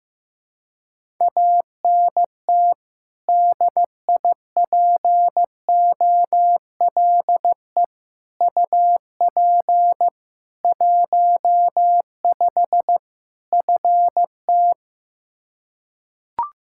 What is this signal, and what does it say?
Morse code, heard as a keyed tone, about 16 wpm, 700 Hz. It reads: ANT DIPOLE UP 15 FT
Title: Practice QSO - Exchange 2 - Info - 15wpm